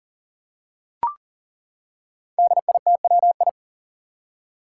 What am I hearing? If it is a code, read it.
Morse code, heard as a keyed tone, about 41 wpm, 700 Hz. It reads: DITWI
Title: Five Challenging Letter-Number Combinations 40wpm